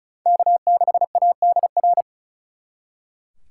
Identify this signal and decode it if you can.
Morse code, heard as a keyed tone, about 46 wpm, 700 Hz. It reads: K6ADR